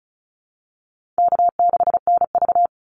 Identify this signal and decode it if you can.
Morse code, heard as a keyed tone, about 34 wpm, 700 Hz. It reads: K6NV
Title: Easy Call Signs Repeated 3x - 35wpm